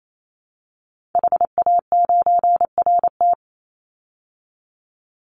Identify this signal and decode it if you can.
Morse code, heard as a keyed tone, about 28 wpm, 700 Hz. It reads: HA9RT
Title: Call Signs Repeated 2x - 28wpm